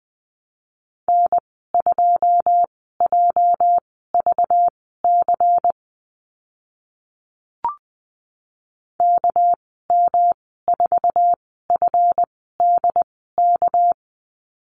Morse code, heard as a keyed tone, about 20 wpm, spaced as for 14 wpm, 700 Hz. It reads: N2JVC KM4FDK
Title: 219 Lesson 35g - Number 0 - Call Signs 20wpm